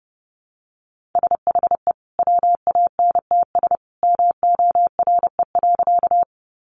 Morse code, heard as a keyed tone, about 30 wpm, 700 Hz. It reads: SHE WANTS MORE.
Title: Easy Phrases 30wpm